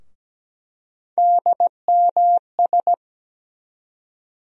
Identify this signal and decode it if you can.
Morse code, heard as a keyed tone, about 17 wpm, 700 Hz. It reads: DMS
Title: USA Call Sign Postfixes - 3 Letters 17wpm